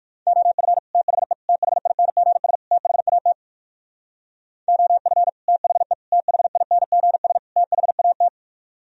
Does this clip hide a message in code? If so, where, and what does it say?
Yes: Morse code from 0.3 to 8.3 s, heard as a keyed tone, about 60 wpm, 700 Hz. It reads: OF THE THINGS THAT OF THE THINGS THAT